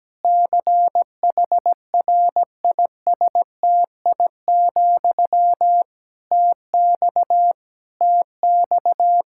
Morse code, heard as a keyed tone, about 17 wpm, 700 Hz. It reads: CHRISTI, TX TX
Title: K1USN SST QSO Practice 17wpm